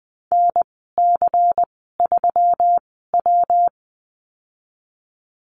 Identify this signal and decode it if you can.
Morse code, heard as a keyed tone, about 19 wpm, spaced as for 14 wpm, 700 Hz. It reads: NC3W
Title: 122 Lesson 20g - Number 3 - Call Signs 20wpm